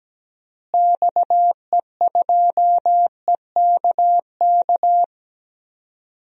Morse code, heard as a keyed tone, about 17 wpm, 700 Hz. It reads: XE2EKK
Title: DXCC-100 Call Signs - 6 Characters - Repeated 2x - 17wpm